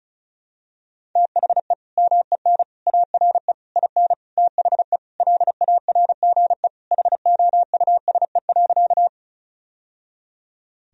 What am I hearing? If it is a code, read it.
Morse code, heard as a keyed tone, about 36 wpm, 700 Hz. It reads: THE MEN ARE IN THE LARGE HOUSE.
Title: Sentences from Top 1000 Words - Review Part 1 - 35wpm